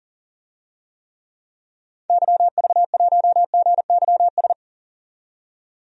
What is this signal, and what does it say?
Morse code, heard as a keyed tone, about 40 wpm, 700 Hz. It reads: YV1GYS
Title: DXCC-100 Call Signs - 6 Characters - 40wpm